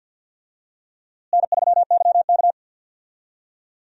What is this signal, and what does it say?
Morse code, heard as a keyed tone, about 48 wpm, 700 Hz. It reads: N3YK
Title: USA Call Signs - 4 Characters - Repeated 2x - 50wpm